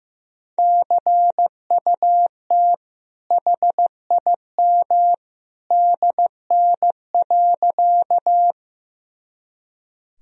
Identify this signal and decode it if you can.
Morse code, heard as a keyed tone, about 15 wpm, 700 Hz. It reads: CUT HIM DN.
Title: Sentences from Top 300 Words and Abbreviations 15wpm